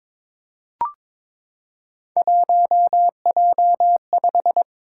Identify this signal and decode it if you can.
Morse code, heard as a keyed tone, about 23 wpm, 700 Hz. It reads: 1J5
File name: practice / 22wpm / Three Letter-Number Combinations - 22wpm